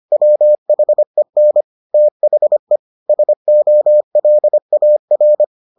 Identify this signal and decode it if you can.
Morse code, heard as a keyed tone, about 25 wpm, 600 Hz. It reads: WHEN THE SOLAR